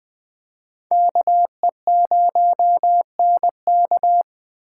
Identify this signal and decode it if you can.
Morse code, heard as a keyed tone, about 24 wpm, 700 Hz. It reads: KE0NK
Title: Speed Racing - USA 2x2 Call Signs 20wpm